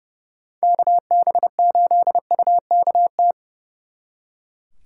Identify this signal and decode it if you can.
Morse code, heard as a keyed tone, about 36 wpm, 700 Hz. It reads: KB8UKT
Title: Speed Racing - USA 2x3 Call Signs 30wpm